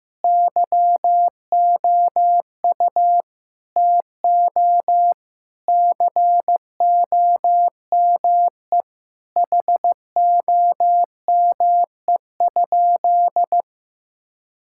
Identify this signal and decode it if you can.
Morse code, heard as a keyed tone, about 15 wpm, 700 Hz. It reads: YOU TO COME HOME?
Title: Sentences from Top 1600 Words 15wpm